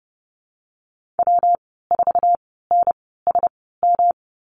Morse code, heard as a keyed tone, about 30 wpm, 700 Hz. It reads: W 4 N S M
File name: Lesson 19g - Letter w - Call Signs 30wpm